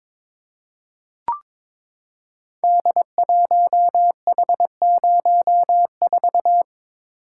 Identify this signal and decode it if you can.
Morse code, heard as a keyed tone, about 22 wpm, 700 Hz. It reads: D1H04
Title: Code Groups - 5 Letter:Numbers - 22wpm